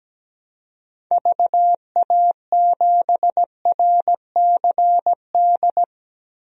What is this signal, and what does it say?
Morse code, heard as a keyed tone, about 17 wpm, 700 Hz. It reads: VA7RCD
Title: DXCC-100 Call Signs - 6 Characters - 17wpm